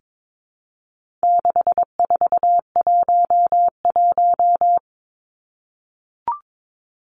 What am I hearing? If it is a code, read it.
Morse code, heard as a keyed tone, about 22 wpm, 700 Hz. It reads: 6411
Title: Numbers - 4-Digit 22wpm